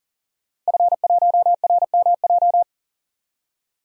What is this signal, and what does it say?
Morse code, heard as a keyed tone, about 40 wpm, 700 Hz. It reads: F1RMJ
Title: DXCC-100 Call Signs - 5 Characters - 40wpm